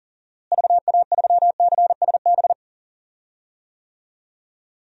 Morse code, heard as a keyed tone, about 39 wpm, 700 Hz. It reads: VA3CSB
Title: DXCC-100 Call Signs - 6 Characters - Repeated 3x - 40wpm